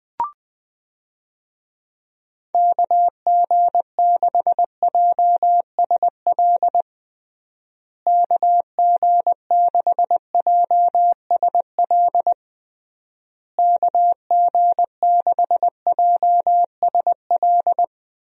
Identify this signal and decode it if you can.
Morse code, heard as a keyed tone, about 20 wpm, 700 Hz. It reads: KG6JSL KG6JSL KG6JSL
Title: USA Call Signs - 6 Characters - Repeated 3x - 20wpm